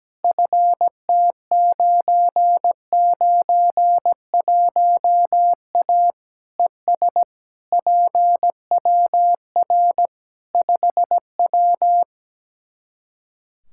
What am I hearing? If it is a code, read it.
Morse code, heard as a keyed tone, about 17 wpm, 700 Hz. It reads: FT991A ES PWR 5W
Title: Practice QSO - Exchange 2 - Info - 17wpm